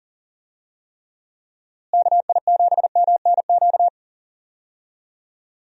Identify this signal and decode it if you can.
Morse code, heard as a keyed tone, about 40 wpm, 700 Hz. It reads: KI7MNQ
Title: USA Call Signs - 6 Characters - Repeated 2x - 40wpm